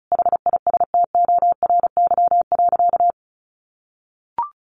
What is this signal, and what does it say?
Morse code, heard as a keyed tone, about 34 wpm, 700 Hz. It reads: HISTORY.
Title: A Sound Of Thunder 35wpm